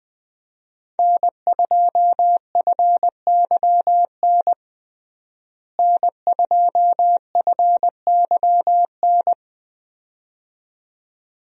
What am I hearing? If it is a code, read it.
Morse code, heard as a keyed tone, about 20 wpm, 700 Hz. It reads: N2FYN N2FYN
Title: USA Call Signs - 5 Characters - Repeated 2x - 20wpm